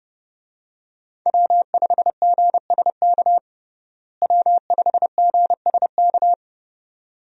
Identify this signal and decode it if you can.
Morse code, heard as a keyed tone, about 30 wpm, 700 Hz. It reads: W5GSK W5GSK
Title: USA Call Signs - 5 Characters - Repeated 2x - 30wpm